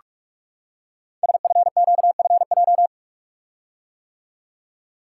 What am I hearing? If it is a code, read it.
Morse code, heard as a keyed tone, about 44 wpm, 700 Hz. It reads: SFQFJ
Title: Code Groups - 5 Letters - 45wpm